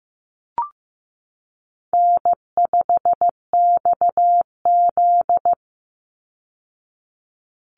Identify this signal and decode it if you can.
Morse code, heard as a keyed tone, about 15 wpm, 700 Hz. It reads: N5XZ
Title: Easy Call Signs Repeated 2x - 15wpm